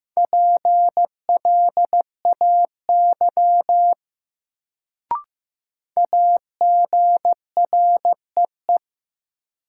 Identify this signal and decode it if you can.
Morse code, heard as a keyed tone, about 15 wpm, 700 Hz. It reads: PLAY AGREE
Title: Sets of 5 Words Encore - New Word Drill - 15wpm